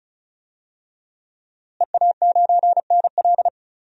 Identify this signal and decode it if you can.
Morse code, heard as a keyed tone, about 36 wpm, 700 Hz. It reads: EA9NL
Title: DXCC-100 Call Signs - 5 Characters - Repeated 3x - 35wpm